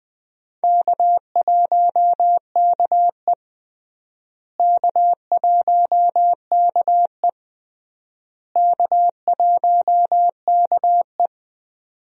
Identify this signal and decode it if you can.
Morse code, heard as a keyed tone, about 20 wpm, 700 Hz. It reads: K1KE K1KE K1KE
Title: USA Call Signs - 4 Characters - Repeated 3x - 20wpm